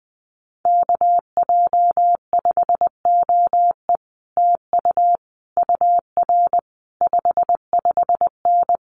Morse code, heard as a keyed tone, about 20 wpm, 700 Hz. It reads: KJ5OE TU UR 55N
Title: Common QSO Phrases Encore 20wpm